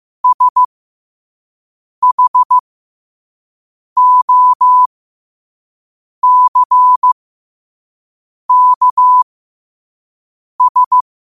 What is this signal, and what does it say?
Morse code, heard as a keyed tone, about 15 wpm, spaced as for 5 wpm, 1000 Hz. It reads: SHOCKS
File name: qotdfiles / 5wpm